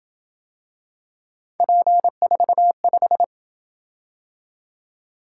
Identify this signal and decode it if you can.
Morse code, heard as a keyed tone, about 27 wpm, 700 Hz. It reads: P45